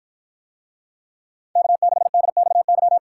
Morse code, heard as a keyed tone, about 53 wpm, 700 Hz. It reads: K6DXY